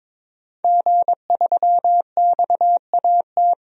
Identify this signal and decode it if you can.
Morse code, heard as a keyed tone, about 22 wpm, 700 Hz. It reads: G3XAT
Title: DXCC-100 Call Signs - 5 Characters - 22wpm